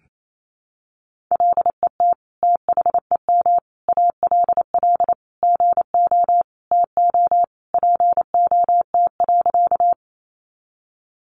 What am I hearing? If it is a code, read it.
Morse code, heard as a keyed tone, about 28 wpm, 700 Hz. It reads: LET THEM ALL GO TO POT.
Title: Sentences from Top 1000 Words 28wpm